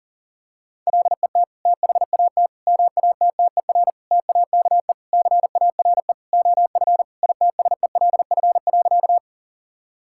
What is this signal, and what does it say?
Morse code, heard as a keyed tone, about 40 wpm, 700 Hz. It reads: LET THAT MATTER TAKE CARE OF ITSELF.
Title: Sentences from Top 1000 Words - Review Part 5 - 40wpm